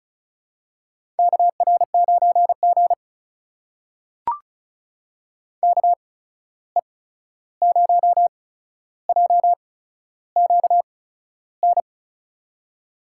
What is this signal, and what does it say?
Morse code, heard as a keyed tone, about 36 wpm, spaced as for 13 wpm, 700 Hz. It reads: KR9G KE0JQN
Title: USA Call Signs - 8x Character Spacing - 35wpm